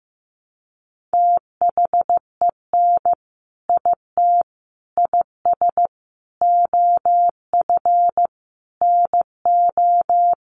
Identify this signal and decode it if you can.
Morse code, heard as a keyed tone, about 15 wpm, 700 Hz. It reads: THEN IT IS OF NO
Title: Sentences from Top 1400 Words 15wpm